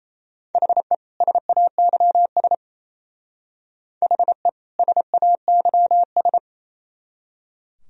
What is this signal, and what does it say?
Morse code, heard as a keyed tone, about 36 wpm, 700 Hz. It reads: HE SAYS HE SAYS